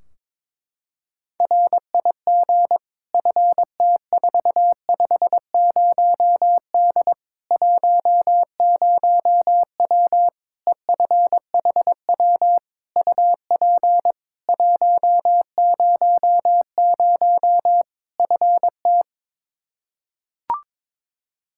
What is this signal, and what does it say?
Morse code, heard as a keyed tone, about 22 wpm, 700 Hz. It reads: RIG FT450D 10W EFHW UP 100 FT
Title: K1USN SST QSO Practice 22wpm